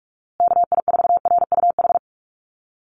Morse code, heard as a keyed tone, about 44 wpm, 700 Hz. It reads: KI4RUH
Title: USA Call Signs Repeated 2x 45wpm